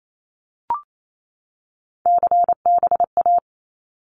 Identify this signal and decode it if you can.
Morse code, heard as a keyed tone, about 28 wpm, 700 Hz. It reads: CBA
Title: Abbreviations 28wpm